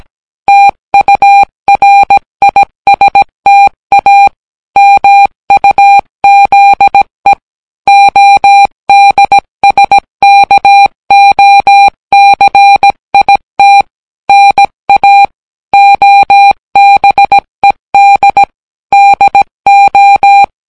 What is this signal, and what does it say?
Morse code, heard as a keyed tone, about 17 wpm, 800 Hz. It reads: TURISTA MUZE ODSKOCIT NA OBED DO